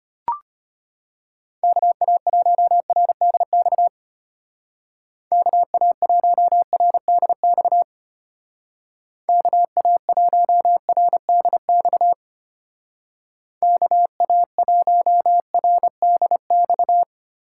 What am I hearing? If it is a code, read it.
Morse code, heard as a keyed tone, about 31 wpm, 700 Hz. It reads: KA1RDX KA1RDX KA1RDX KA1RDX